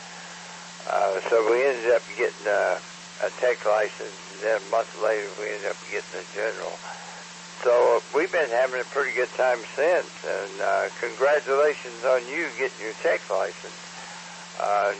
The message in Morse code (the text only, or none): none